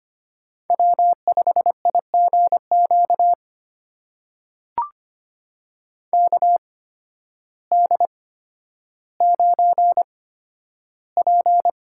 Morse code, heard as a keyed tone, about 25 wpm, spaced as for 12 wpm, 700 Hz. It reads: W5IGQ KD9P
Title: USA Call Signs - 5 Characters - 8x Character Spacing - 25wpm